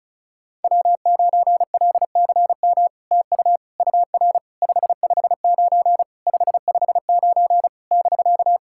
Morse code, heard as a keyed tone, about 36 wpm, 700 Hz. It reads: W9LCM TU UR 559 559 <BK>
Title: SOTA QSO Practice 35wpm